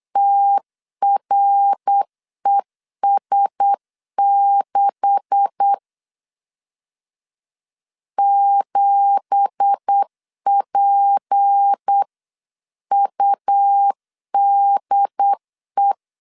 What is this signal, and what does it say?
Morse code, heard as a keyed tone, about 8 wpm, 800 Hz. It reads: TRES6 7PUDE
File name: CW / DET / 1 9 1500C S B 8wpm 5624